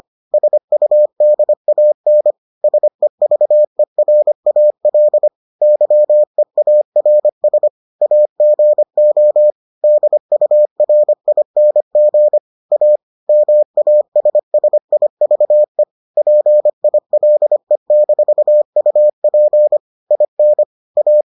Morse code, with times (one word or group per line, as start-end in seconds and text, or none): 0.3-2.3 SUDAN
2.6-5.3 SEVERAL
5.6-7.7 YEARS
8.0-9.5 AGO
9.8-12.4 DURING
12.7-13.0 A
13.3-15.8 MASSIVE
16.2-19.8 PILE-UP
20.1-20.6 IN
21.0-21.2 A